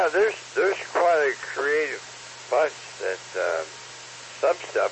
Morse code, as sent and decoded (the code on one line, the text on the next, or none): none